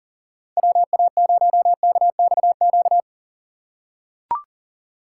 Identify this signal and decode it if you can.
Morse code, heard as a keyed tone, about 39 wpm, 700 Hz. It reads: WA0KXQ
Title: USA Call Signs - 6 Characters - 3x Character Spacing - 40wpm